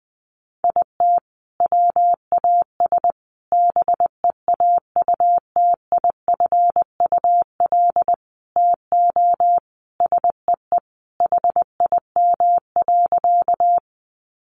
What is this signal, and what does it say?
Morse code, heard as a keyed tone, about 20 wpm, 700 Hz. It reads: IT WAS BEAUTIFUL TO SEE HIM.
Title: Sentences from Top 900 Words 20wpm